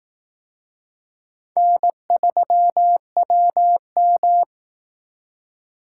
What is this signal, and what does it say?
Morse code, heard as a keyed tone, about 18 wpm, 700 Hz. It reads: N3WM
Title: Speed Racing - USA 1x2 Call Signs 15wpm